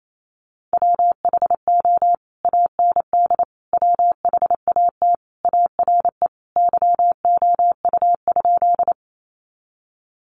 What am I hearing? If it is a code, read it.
Morse code, heard as a keyed tone, about 28 wpm, 700 Hz. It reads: WHO AND WHAT ARE YOU?
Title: Sentences from Top 100 Words 28wpm